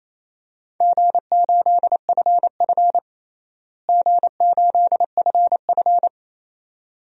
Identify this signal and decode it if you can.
Morse code, heard as a keyed tone, about 27 wpm, 700 Hz. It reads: G8FF G8FF
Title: DXCC-100 Call Signs - 4 Characters - Repeated 2x - 28wpm